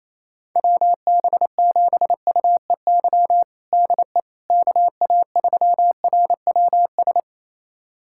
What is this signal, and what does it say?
Morse code, heard as a keyed tone, about 28 wpm, 700 Hz. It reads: WB7UEY DE KA3RWS
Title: USA Callsign DE Callsign - 6 Characters - 28wpm